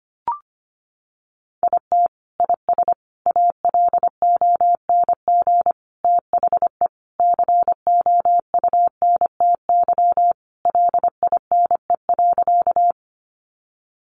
Morse code, heard as a keyed tone, about 25 wpm, 700 Hz. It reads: IT IS ALONG THE COUNTY LINE.
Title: Sentences from Top 1200 Words 25wpm